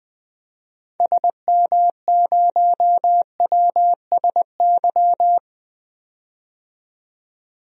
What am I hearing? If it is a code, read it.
Morse code, heard as a keyed tone, about 20 wpm, 700 Hz. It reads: SM0WSY
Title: DXCC-100 Call Signs - 6 Characters - 20wpm